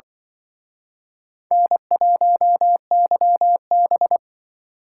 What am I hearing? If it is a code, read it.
Morse code, heard as a keyed tone, about 24 wpm, 700 Hz. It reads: N1YB
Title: Speed Racing - USA Call Signs 20wpm